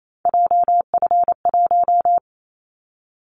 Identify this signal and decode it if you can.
Morse code, heard as a keyed tone, about 28 wpm, 700 Hz. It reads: JF1
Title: Callsign Prefixes for Top 25 DXCC - 28wpm